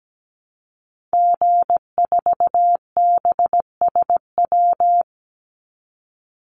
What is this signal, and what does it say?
Morse code, heard as a keyed tone, about 17 wpm, 700 Hz. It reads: G4BSW
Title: Call Signs Repeated 2x - 17wpm